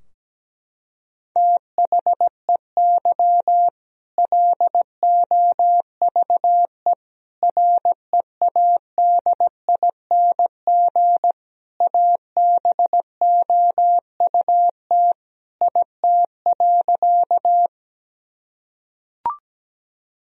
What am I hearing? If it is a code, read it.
Morse code, heard as a keyed tone, about 17 wpm, 700 Hz. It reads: THEY LOVE READING ABOUT IT.
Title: Sentences from Top 1000 Words - Review Part 3 - 17wpm